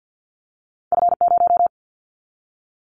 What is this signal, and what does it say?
Morse code, heard as a keyed tone, about 50 wpm, 700 Hz. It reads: F0
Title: Callsign Prefixes for Top 25 DXCC - 50wpm